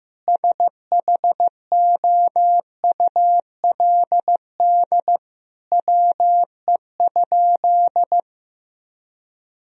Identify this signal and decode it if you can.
Morse code, heard as a keyed tone, about 15 wpm, 700 Hz. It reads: SHOULD WE?
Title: Sentences from Top 1000 Words - Review Part 3 - 15wpm